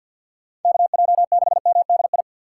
Morse code, heard as a keyed tone, about 47 wpm, 700 Hz. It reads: KJ6MDI
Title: USA Call Signs - 6 Characters - 3x Character Spacing - 50wpm